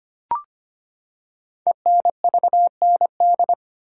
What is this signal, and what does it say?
Morse code, heard as a keyed tone, about 25 wpm, 700 Hz. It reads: ENVND
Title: Code Groups - 5 Letters - 25wpm